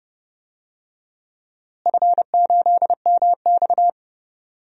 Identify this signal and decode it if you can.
Morse code, heard as a keyed tone, about 30 wpm, 700 Hz. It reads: F8MX
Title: DXCC-100 Call Signs - 4 Characters - Repeated 3x - 30wpm